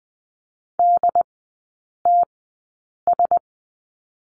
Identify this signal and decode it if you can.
Morse code, heard as a keyed tone, about 20 wpm, spaced as for 8 wpm, 700 Hz. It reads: DTS
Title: 093 Lesson 16e - Letter c - Three Characters 20wpm